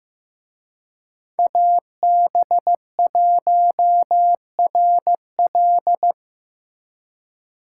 Morse code, heard as a keyed tone, about 18 wpm, 700 Hz. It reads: AB1RL
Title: Speed Racing - USA Call Signs 15wpm